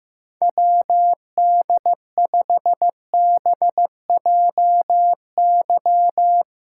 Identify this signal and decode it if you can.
Morse code, heard as a keyed tone, about 15 wpm, 700 Hz. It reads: WD5BJY
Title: USA Callsign DE Callsign - 6 Characters - 15wpm